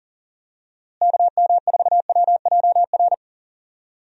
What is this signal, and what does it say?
Morse code, heard as a keyed tone, about 40 wpm, 700 Hz. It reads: KM4WJR
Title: USA Call Signs - 6 Characters - Repeated 2x - 40wpm